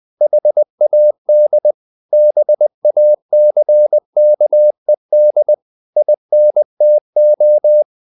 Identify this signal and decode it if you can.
Morse code, heard as a keyed tone, about 20 wpm, 600 Hz. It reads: HAD BACKED INTO
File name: the old man and the sea / the old man and the sea part 1 20wpm